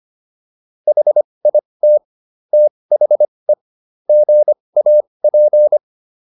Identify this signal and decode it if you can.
Morse code, heard as a keyed tone, about 25 wpm, spaced as for 20 wpm, 600 Hz. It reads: HIT THE GAP